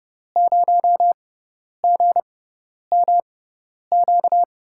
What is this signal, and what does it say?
Morse code, heard as a keyed tone, about 30 wpm, spaced as for 10 wpm, 700 Hz. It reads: 0GMQ
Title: USA Call Signs - 6 Characters - 6x Character Spacing - 30wpm